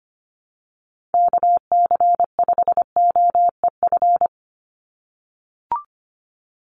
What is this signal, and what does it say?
Morse code, heard as a keyed tone, about 25 wpm, 700 Hz. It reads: KC5OEF
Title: USA Call Signs 25wpm